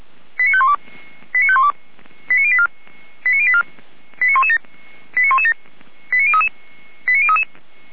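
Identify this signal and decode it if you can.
Five-tone selective call (ZVEI), heard as quick runs of five tones, tones of about 80 ms. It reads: 89512 89512 89084 89084 89117 89117 89033 89033
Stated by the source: ZVEI five-tone selective call received on SDR.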